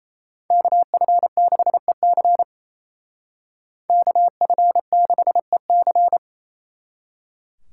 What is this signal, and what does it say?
Morse code, heard as a keyed tone, about 33 wpm, 700 Hz. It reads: KF6EC KF6EC
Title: Speed Racing - USA Call Signs 28wpm